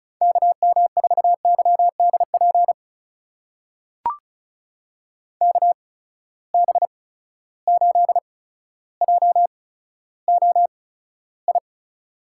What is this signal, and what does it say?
Morse code, heard as a keyed tone, about 36 wpm, spaced as for 15 wpm, 700 Hz. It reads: KM4YDP KB8JOI
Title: USA Call Signs - 6 Characters - 8x Character Spacing - 35wpm